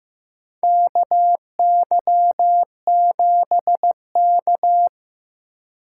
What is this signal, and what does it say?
Morse code, heard as a keyed tone, about 15 wpm, 700 Hz. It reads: KY7K
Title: K1USN SST QSO Practice - 15wpm